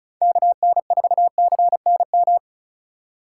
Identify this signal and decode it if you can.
Morse code, heard as a keyed tone, about 34 wpm, 700 Hz. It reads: KN4CNM
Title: USA Call Signs - 6 Characters - 6x Character Spacing - 35wpm